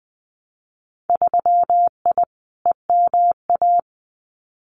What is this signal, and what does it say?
Morse code, heard as a keyed tone, about 20 wpm, 700 Hz. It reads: 3I EMA
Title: Winter Field Day Exchanges Encore 20wpm